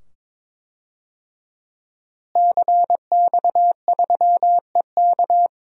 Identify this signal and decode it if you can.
Morse code, heard as a keyed tone, about 21 wpm, 700 Hz. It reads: CX3EK
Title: DXCC-100 Call Signs - 5 Characters - Repeated 2x - 22wpm